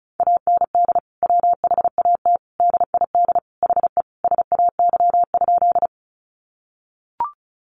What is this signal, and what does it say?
Morse code, heard as a keyed tone, about 34 wpm, 700 Hz. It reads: AND WHAT DID HE SAY?
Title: Sentences from Top 200 Words 35wpm